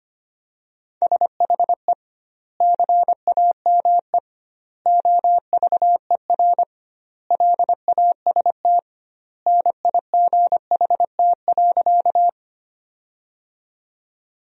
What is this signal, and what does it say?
Morse code, heard as a keyed tone, about 25 wpm, 700 Hz. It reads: SHE CAME OVER LAST NIGHT.
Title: 2x Word-Spacing - Sentences from Top 300 Words - 25wpm